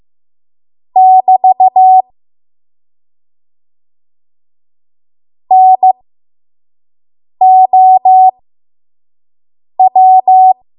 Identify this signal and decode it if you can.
Morse code, heard as a keyed tone, about 15 wpm, spaced as for 5 wpm, 750 Hz. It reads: <BT> NOW
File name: Morse / 05WPM